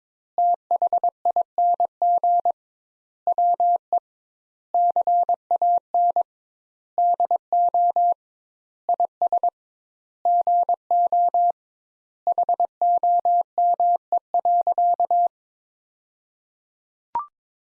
Morse code, heard as a keyed tone, about 22 wpm, 700 Hz. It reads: THING WE CAN DO IS GO HOME.
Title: Sentences Top 200 Words - 22wpm